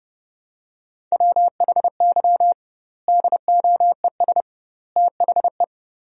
Morse code, heard as a keyed tone, about 30 wpm, 700 Hz. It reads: WHY DOES THE
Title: 2x Word-Spacing - Sentences from Top 500 Words - 30wpm